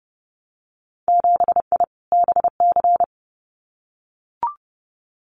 Morse code, heard as a keyed tone, about 30 wpm, 700 Hz. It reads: 7I BC
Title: Winter Field Day Exchanges Encore 30wpm